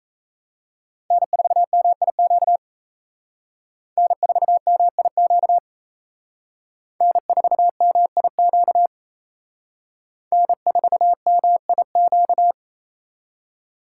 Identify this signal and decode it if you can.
Morse code, heard as a keyed tone, about 35 wpm, 700 Hz. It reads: N4MIQ N4MIQ N4MIQ N4MIQ
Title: Speed Racing - USA 1x3 Call Signs 28wpm